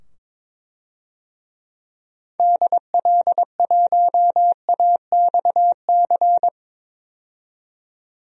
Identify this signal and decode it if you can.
Morse code, heard as a keyed tone, about 21 wpm, 700 Hz. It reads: DL1AXC
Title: DXCC-100 Call Signs - 6 Characters - 22wpm